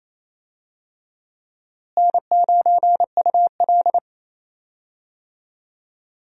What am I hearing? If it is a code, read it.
Morse code, heard as a keyed tone, about 28 wpm, 700 Hz. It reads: N9UL
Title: USA Call Signs - 4 Characters - Repeated 3x - 28wpm